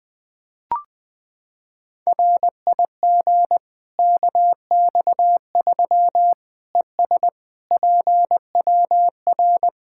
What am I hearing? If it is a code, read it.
Morse code, heard as a keyed tone, about 20 wpm, 700 Hz. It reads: RIG KX3 ES PWR
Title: K1USN SST QSO Practice 20wpm